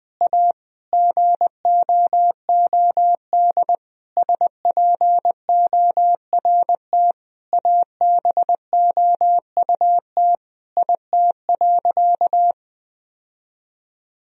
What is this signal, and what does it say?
Morse code, heard as a keyed tone, about 20 wpm, 700 Hz. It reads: A GOOD SPORT ABOUT IT.
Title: Sentences from Top 1000 Words - Review Part 4 - 20wpm